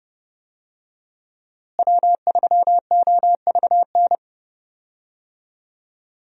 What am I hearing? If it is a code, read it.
Morse code, heard as a keyed tone, about 31 wpm, 700 Hz. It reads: W3OVN
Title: USA Call Signs - 5 Characters - Repeated 3x - 30wpm